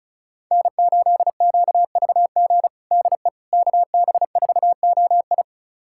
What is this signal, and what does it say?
Morse code, heard as a keyed tone, about 36 wpm, 700 Hz. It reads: N8QVG DE KB4OI
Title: USA Callsign DE Callsign - 5 Characters - 35wpm